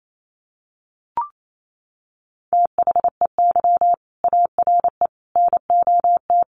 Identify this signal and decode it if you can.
Morse code, heard as a keyed tone, about 28 wpm, 700 Hz. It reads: THEY ARE NOT
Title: Sentences from Top 1200 Words 28wpm